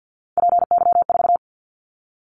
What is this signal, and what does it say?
Morse code, heard as a keyed tone, about 50 wpm, 700 Hz. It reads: PY4
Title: Callsign Prefixes for Top 100 DXCC - 50wpm